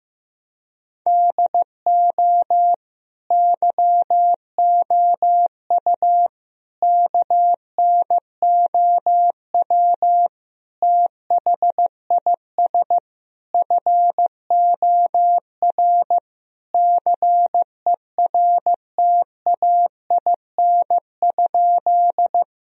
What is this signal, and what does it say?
Morse code, heard as a keyed tone, about 16 wpm, 700 Hz. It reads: DO YOU KNOW THIS FOR CERTAIN?
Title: Sentences from Top 1000 Words - Review Part 2 - 15wpm